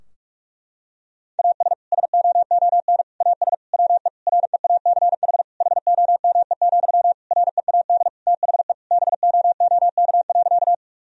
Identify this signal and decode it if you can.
Morse code, heard as a keyed tone, about 46 wpm, 700 Hz. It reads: AS SOON AS WE REACH HOME, READ THE BOOK.
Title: Sentences from Top 1000 Words - Review Part 2 - 45wpm